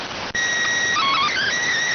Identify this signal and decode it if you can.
SSTV (slow-scan television), PD 290.